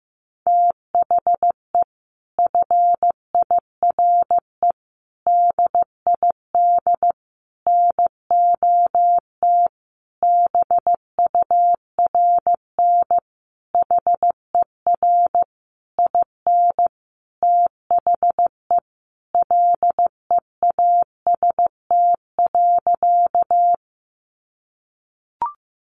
Morse code, heard as a keyed tone, about 15 wpm, 700 Hz. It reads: THE FIRE DID NOT BURN HER IN THE LEAST.
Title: Sentences from Top 1300 Words 15wpm